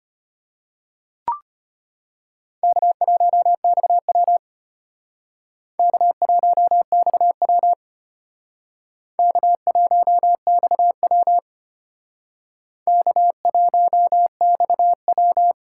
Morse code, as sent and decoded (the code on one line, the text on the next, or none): -.- .---- -..- .-- / -.- .---- -..- .-- / -.- .---- -..- .-- / -.- .---- -..- .--
K1XW K1XW K1XW K1XW